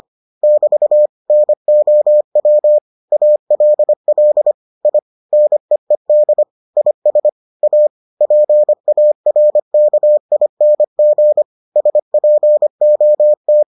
Morse code, heard as a keyed tone, about 25 wpm, 600 Hz. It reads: <BT> NOW ALL I NEED IS A PARKING SPOT